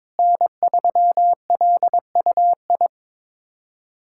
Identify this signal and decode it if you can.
Morse code, heard as a keyed tone, about 22 wpm, 700 Hz. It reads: N3LUI
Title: USA Callsign DE Callsign - 5 Characters - 22wpm